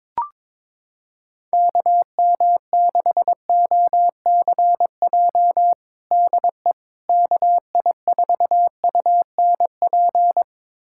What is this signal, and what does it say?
Morse code, heard as a keyed tone, about 22 wpm, 700 Hz. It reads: KM6OCJ DE KI4UNP
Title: USA Callsign DE Callsign - 6 Characters - 22wpm